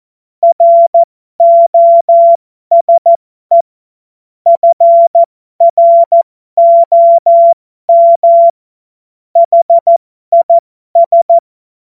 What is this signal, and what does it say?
Morse code, heard as a keyed tone, about 14 wpm, spaced as for 12 wpm, 680 Hz. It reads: ROSE FROM HIS